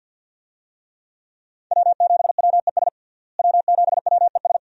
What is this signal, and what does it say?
Morse code, heard as a keyed tone, about 53 wpm, 700 Hz. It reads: W7WES W7WES
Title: USA Call Signs - 5 Characters - Repeated 2x - 50wpm